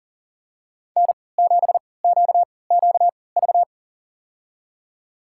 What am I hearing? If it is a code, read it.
Morse code, heard as a keyed tone, about 40 wpm, spaced as for 23 wpm, 700 Hz. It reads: N7QQV
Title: USA Call Signs - 5 Characters - 3x Character Spacing - 40wpm